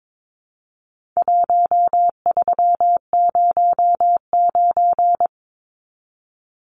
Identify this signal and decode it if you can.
Morse code, heard as a keyed tone, about 22 wpm, 700 Hz. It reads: 1309
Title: Numbers - 4-Digit 22wpm